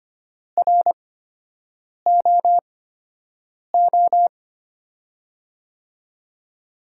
Morse code, heard as a keyed tone, about 25 wpm, spaced as for 7 wpm, 700 Hz. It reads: ROO
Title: USA Call Signs - 6 Characters - 8x Character Spacing - 25wpm